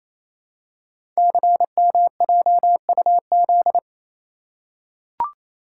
Morse code, heard as a keyed tone, about 27 wpm, 700 Hz. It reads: CMJUZ
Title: Code Groups - 5 Letters - 28wpm